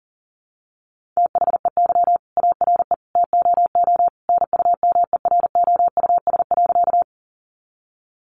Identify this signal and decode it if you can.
Morse code, heard as a keyed tone, about 40 wpm, 700 Hz. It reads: THEY ARE TOO NUMEROUS.
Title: Sentences from Top 1400 Words 40wpm